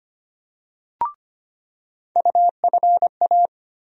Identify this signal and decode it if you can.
Morse code, heard as a keyed tone, about 25 wpm, 700 Hz. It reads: UFA
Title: USA Call Sign Postfixes - 3 Letters 25wpm